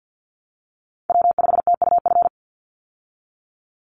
Morse code, heard as a keyed tone, about 50 wpm, 700 Hz. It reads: W5TUF
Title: Call Signs Repeated 3x - 50wpm